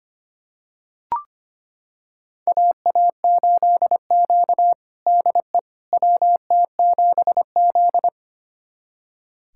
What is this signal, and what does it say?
Morse code, heard as a keyed tone, about 25 wpm, 700 Hz. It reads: AA8Q DE WT7Z
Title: USA Callsign DE Callsign - 4 Characters - 25wpm